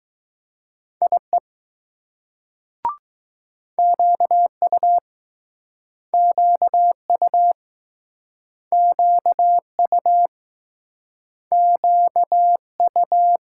Morse code, heard as a keyed tone, about 20 wpm, 700 Hz. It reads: IE QU QU QU QU